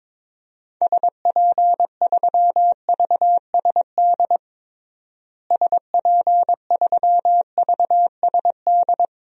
Morse code, heard as a keyed tone, about 22 wpm, 700 Hz. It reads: SP3VSD SP3VSD
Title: DXCC-100 Call Signs - 6 Characters - Repeated 3x - 22wpm